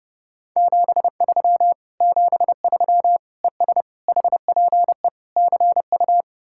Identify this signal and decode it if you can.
Morse code, heard as a keyed tone, about 39 wpm, 700 Hz. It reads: 73 73 ES HPE CU